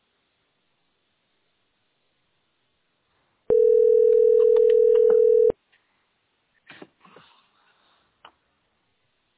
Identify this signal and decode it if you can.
Ringback tone (North America), 440 + 480 Hz.
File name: b1eff4a3-d2e9-4c2f-8940